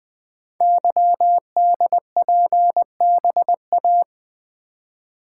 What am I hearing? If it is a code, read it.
Morse code, heard as a keyed tone, about 20 wpm, 700 Hz. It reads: YDPBA
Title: Code Groups - 5 Letters - 20wpm